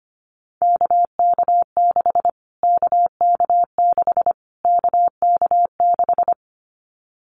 Morse code, heard as a keyed tone, about 25 wpm, 700 Hz. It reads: KK6 KK6 KK6
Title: Callsign Prefixes for Top 25 DXCC - Repeated 3x - 25wpm